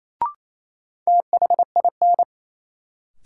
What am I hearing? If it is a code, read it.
Morse code, heard as a keyed tone, about 28 wpm, 700 Hz. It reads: THIN
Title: Sentences from Top 1200 Words - New Word Drill - 28wpm